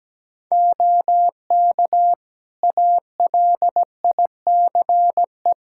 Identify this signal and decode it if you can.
Morse code, heard as a keyed tone, about 17 wpm, 700 Hz. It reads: OK ALICE
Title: Practice QSO - Exchange 3 - More Info - 17wpm